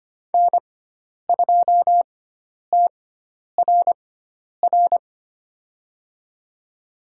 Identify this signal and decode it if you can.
Morse code, heard as a keyed tone, about 25 wpm, spaced as for 10 wpm, 700 Hz. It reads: N2TRR
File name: Lesson 16g - Letter c - Call Signs 25wpm